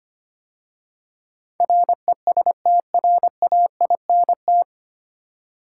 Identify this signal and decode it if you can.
Morse code, heard as a keyed tone, about 25 wpm, 700 Hz. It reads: RESTRAINT
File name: practice / 25wpm / 9 Letter Words - First 3 Letters Given - 25wpm